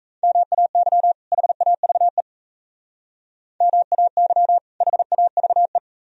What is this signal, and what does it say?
Morse code, heard as a keyed tone, about 36 wpm, 700 Hz. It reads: MAY HAVE MAY HAVE